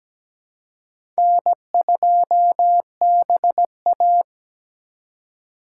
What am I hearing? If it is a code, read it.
Morse code, heard as a keyed tone, about 17 wpm, 700 Hz. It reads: N2BA
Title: USA Call Signs - 4 Characters - 3x Character Spacing - 17wpm